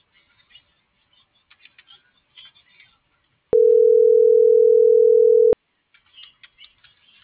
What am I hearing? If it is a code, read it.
Ringback tone (North America), 440 + 480 Hz.